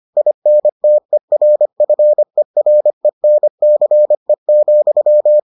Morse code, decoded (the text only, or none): INTERFERENCE,